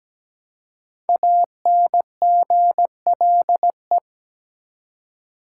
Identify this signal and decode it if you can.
Morse code, heard as a keyed tone, about 17 wpm, 700 Hz. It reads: ANGLE
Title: Five Letter Words - 17wpm